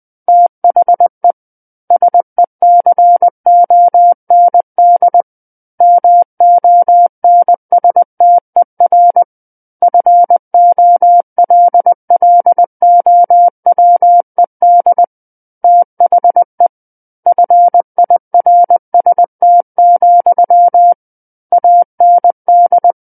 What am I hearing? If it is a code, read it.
Morse code, heard as a keyed tone, about 20 wpm, 700 Hz. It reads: THE SECOND MONSTER FOLLOWED THE FIRST, AND
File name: 20 WPM